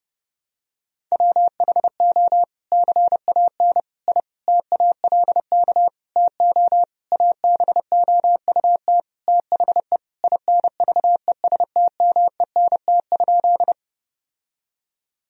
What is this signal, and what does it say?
Morse code, heard as a keyed tone, about 30 wpm, 700 Hz. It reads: WHO CAN I TALK TO ABOUT THE INVESTMENT?
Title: Sentences from Top 1000 Words - Review Part 4 - 30wpm